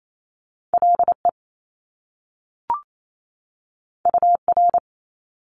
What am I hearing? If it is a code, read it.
Morse code, heard as a keyed tone, about 28 wpm, 700 Hz. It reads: LE UR
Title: Top Two Letter Combinations in Words 28wpm